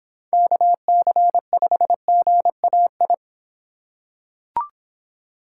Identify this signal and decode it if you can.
Morse code, heard as a keyed tone, about 26 wpm, 700 Hz. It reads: KC5GAI